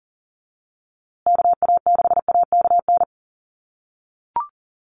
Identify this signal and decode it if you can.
Morse code, heard as a keyed tone, about 40 wpm, 700 Hz. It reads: KA6AKN
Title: USA Call Signs Repeated 3x 40wpm